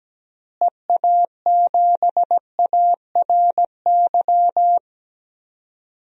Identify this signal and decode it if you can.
Morse code, heard as a keyed tone, about 17 wpm, 700 Hz. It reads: EA7ARY
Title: DXCC-100 Call Signs - 6 Characters - Repeated 3x - 17wpm